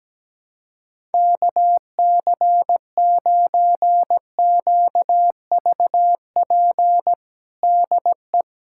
Morse code, heard as a keyed tone, about 17 wpm, 700 Hz. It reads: KC9QVP DE
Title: USA Callsign DE Callsign - 6 Characters - 17wpm